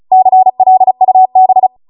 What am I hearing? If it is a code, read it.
Morse code, heard as a keyed tone, about 36 wpm, 750 Hz. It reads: CLUB